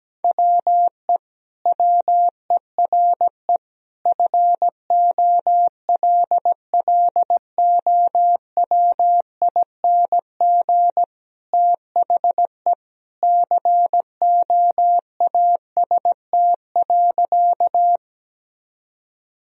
Morse code, heard as a keyed tone, about 17 wpm, 700 Hz. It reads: WE WERE FOLLOWING THE COAST.
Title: Sentences from Top 1000 Words - Review Part 4 - 17wpm